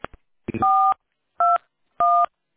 DTMF, heard as telephone touch tones, tones of about 270 ms. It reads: 421